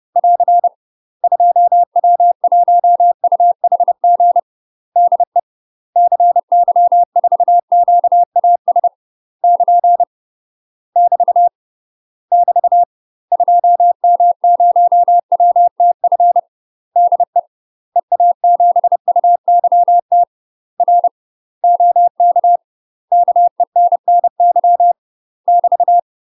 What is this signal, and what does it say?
Morse code, heard as a keyed tone, about 30 wpm, 700 Hz. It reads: <AR> 2W1UHG DE CY4QAS <KN> <BT> <BT> 2M0WTF DE EA7UYT R OK KENNY <BT>
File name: UK MASTER QSOs TRAINING / 30WPM / UK MASTER QSOs 0049